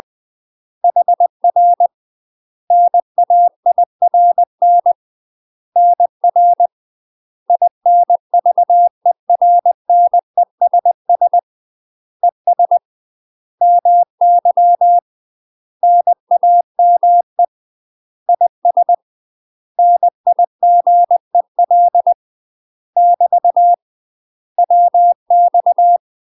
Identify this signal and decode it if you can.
Morse code, heard as a keyed tone, about 20 wpm, 700 Hz. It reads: HR NAIRN NR INVERNESS ES MY NAME IS NIGEL <BT> WX
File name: UK MASTER QSOs TRAINING/20WPM